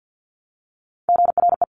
Morse code, heard as a keyed tone, about 50 wpm, 700 Hz. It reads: GRE
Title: Top Three Letter Combinations in Words 50wpm